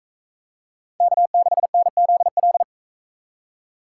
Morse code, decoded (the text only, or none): K6NZL